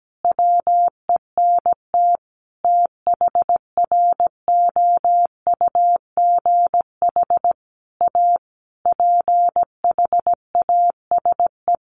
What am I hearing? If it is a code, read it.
Morse code, heard as a keyed tone, about 17 wpm, 700 Hz. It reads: WENT THROUGH A PHASE
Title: Sentences from Top 1200 Words 17wpm